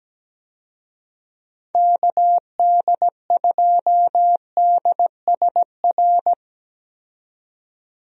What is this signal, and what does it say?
Morse code, heard as a keyed tone, about 17 wpm, 700 Hz. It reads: KD2DSR
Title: USA Call Signs - 6 Characters - 17wpm